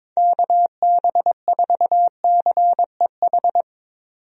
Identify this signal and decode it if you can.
Morse code, heard as a keyed tone, about 22 wpm, 700 Hz. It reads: KB4CEH
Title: USA Call Signs - 6 Characters - 22wpm